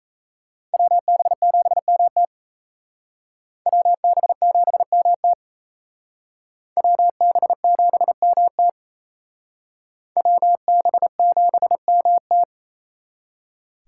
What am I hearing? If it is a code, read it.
Morse code, heard as a keyed tone, about 34 wpm, 700 Hz. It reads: WB7MT WB7MT WB7MT WB7MT
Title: Speed Racing - USA 2x2 Call Signs 28wpm